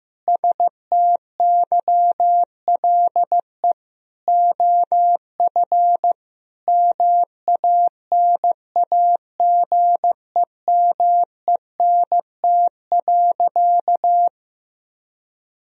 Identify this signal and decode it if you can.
Morse code, heard as a keyed tone, about 14 wpm, 700 Hz. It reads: STYLE OF MANAGEMENT.
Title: Sentences from Top 1000 Words - Review Part 4 - 15wpm